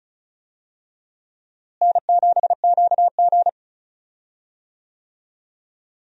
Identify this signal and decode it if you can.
Morse code, heard as a keyed tone, about 36 wpm, 700 Hz. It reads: N7QG
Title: USA Call Signs - 4 Characters - Repeated 2x - 35wpm